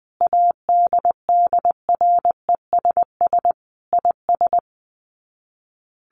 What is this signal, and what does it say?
Morse code, heard as a keyed tone, about 20 wpm, 700 Hz. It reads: ADDRESS IS
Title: Sets of 5 Words Encore2 20wpm